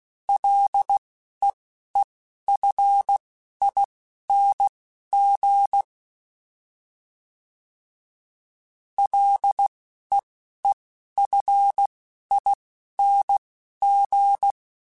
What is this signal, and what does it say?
Morse code, heard as a keyed tone, about 16 wpm, spaced as for 12 wpm, 800 Hz. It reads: LEEFING LEEFING
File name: Ls / Single L m3s / Single L Prefixe 3x2 Twice @10